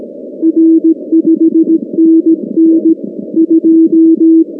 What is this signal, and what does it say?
Morse code, heard as a keyed tone, about 18 wpm, 320 Hz. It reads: R 5NN 2
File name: contests / 2019-cw-wpx-kt5j-k5pi- -2330 0-19